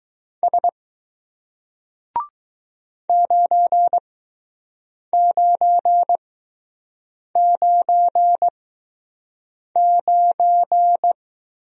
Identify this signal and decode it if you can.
Morse code, heard as a keyed tone, about 20 wpm, spaced as for 6 wpm, 700 Hz. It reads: S 9999